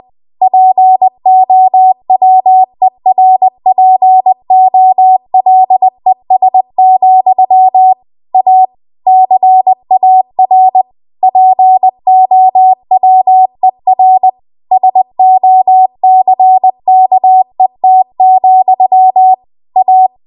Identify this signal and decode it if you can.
Morse code, heard as a keyed tone, about 20 wpm, 750 Hz. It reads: POWERPOLES, A CAR POWER SOCKET, A
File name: Archive / 20 WPM / 20WPM